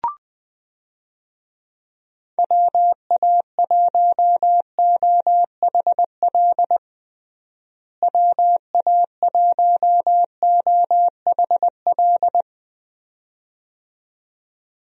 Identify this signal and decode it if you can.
Morse code, heard as a keyed tone, about 21 wpm, 700 Hz. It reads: WA1OHL WA1OHL
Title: USA Call Signs - 6 Characters - Repeated 2x - 20wpm